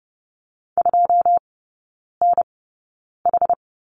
Morse code, heard as a keyed tone, about 32 wpm, spaced as for 9 wpm, 700 Hz. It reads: 2NH
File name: 081 Lesson 14e - Number 2 - Three Characters 30wpm